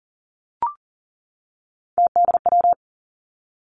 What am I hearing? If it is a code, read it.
Morse code, heard as a keyed tone, about 41 wpm, 700 Hz. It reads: TDW
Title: Three Challenging Letter-Number Combinations 40wpm